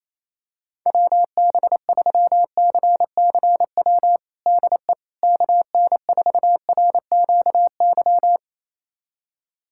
Morse code, heard as a keyed tone, about 28 wpm, 700 Hz. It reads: WB3CCW DE KN4RQY
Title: USA Callsign DE Callsign - 6 Characters - 28wpm